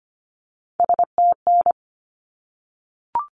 Morse code, heard as a keyed tone, about 25 wpm, 700 Hz. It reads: STN
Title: Abbreviations 25wpm